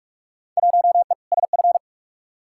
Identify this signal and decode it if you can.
Morse code, heard as a keyed tone, about 44 wpm, 700 Hz. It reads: 1E SF
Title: Field Day Exchanges - 45wpm